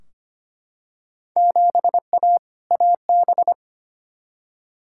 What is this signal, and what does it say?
Morse code, heard as a keyed tone, about 25 wpm, 700 Hz. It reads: 7A AB
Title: Field Day Exchanges Encore - 25wpm